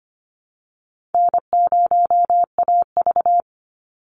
Morse code, heard as a keyed tone, about 25 wpm, 700 Hz. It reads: N0AV
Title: Easy Call Signs Repeated 2x - 25wpm